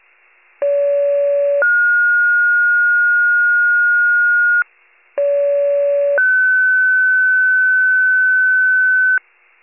Two-tone page, heard as a one-second tone, then a three-second tone, about 570 then 1430 Hz, about 570 then 1500 Hz. It is heard from 0.6 to 9.2 s.